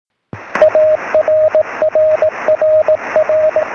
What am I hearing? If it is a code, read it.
Morse code, heard as a keyed tone, about 18 wpm, 620 Hz, with radio noise. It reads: ARRRR
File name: sm7lcb / 13-sk7mw